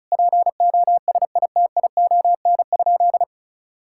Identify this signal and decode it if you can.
Morse code, heard as a keyed tone, about 36 wpm, 700 Hz. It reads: POSITION?
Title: Sentences from Top 1000 Words - Review Part 5 - 35wpm